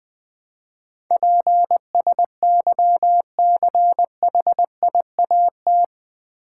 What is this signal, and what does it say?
Morse code, heard as a keyed tone, about 19 wpm, 700 Hz. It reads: PSYCHIAT
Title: Build A Word - 4 Letters At A Time - 11-Letter Words - 20wpm